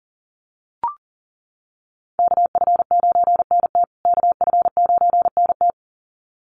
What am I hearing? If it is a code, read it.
Morse code, heard as a keyed tone, about 40 wpm, 700 Hz. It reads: KF9NT KF9NT
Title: USA Call Signs Repeated 2x 40wpm